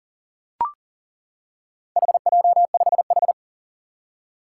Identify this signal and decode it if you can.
Morse code, heard as a keyed tone, about 38 wpm, 700 Hz. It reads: HJ5H
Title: Four Challenging Letter-Number Combinations 40wpm